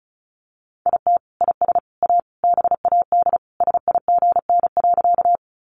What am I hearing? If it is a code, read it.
Morse code, heard as a keyed tone, about 34 wpm, 700 Hz. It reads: IT IS A BAD SIGN.
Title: Sentences from Top 1000 Words 35wpm